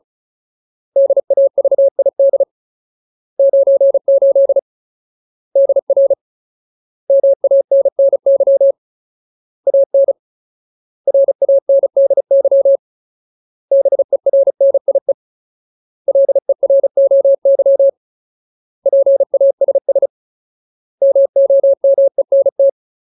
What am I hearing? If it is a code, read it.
Morse code, heard as a keyed tone, about 34 wpm, 550 Hz. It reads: DAVID 98 DR MANNY AN RANDY BERNIE LEROY PASS MOMENT
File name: Ultimate Word List / 35WPM Ultimate